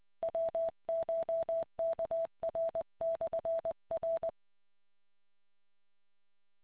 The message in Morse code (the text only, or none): WCHKR/R